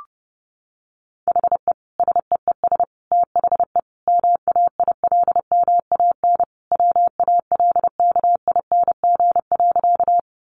Morse code, heard as a keyed tone, about 30 wpm, 700 Hz. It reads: HE SEES THE MAILMAN WALKING.
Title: Easy Phrases 30wpm